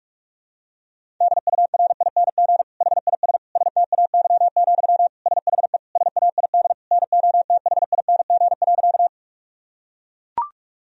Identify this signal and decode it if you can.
Morse code, heard as a keyed tone, about 42 wpm, 700 Hz. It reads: DURING HIS STAY, SHE SAID NOTHING.
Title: Sentences from Top 1000 Words - Review Part 2 - 45wpm